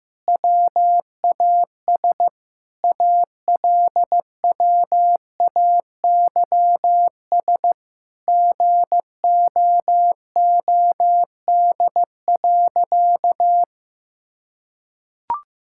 Morse code, heard as a keyed tone, about 15 wpm, 700 Hz. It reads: WAS ALWAYS GOOD.
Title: Sentences from Top 1000 Words - Review Part 5 - 15wpm